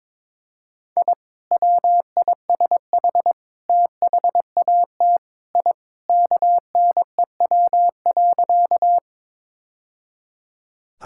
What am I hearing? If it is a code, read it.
Morse code, heard as a keyed tone, about 22 wpm, 700 Hz. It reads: I WISH THAT I KNEW.
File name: practice / 22wpm / Sentences with 2-Word Phrases - 22wpm